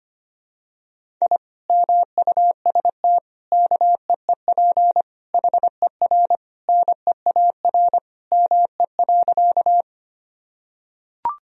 Morse code, heard as a keyed tone, about 25 wpm, 700 Hz. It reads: I MUST KEEP HER NEAR ME.
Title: Sentences from Top 1000 Words - Review Part 1 - 25wpm